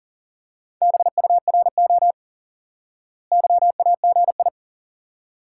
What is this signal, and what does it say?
Morse code, heard as a keyed tone, about 40 wpm, 700 Hz. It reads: BURO YAGI
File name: IQE Course 3 - Lesson 89 - YAGI - Two Elements - 40wpm